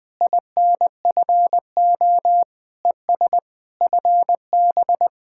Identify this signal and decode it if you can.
Morse code, heard as a keyed tone, about 20 wpm, 700 Hz. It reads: INFO ES FB
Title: Speed-Racing Common QSO Phrases - 20wpm